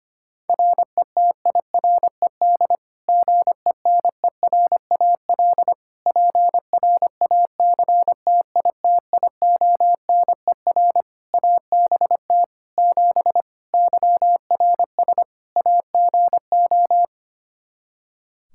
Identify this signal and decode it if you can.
Morse code, heard as a keyed tone, about 25 wpm, 700 Hz. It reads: RETIRED GENERAL PRACTITIONER ABT 7 YRS AGO
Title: Practice QSO - Exchange 3 - More Info - 25wpm